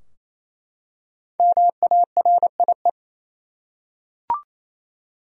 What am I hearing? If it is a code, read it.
Morse code, heard as a keyed tone, about 28 wpm, 700 Hz. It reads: MARIE
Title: USA Common Names - 28wpm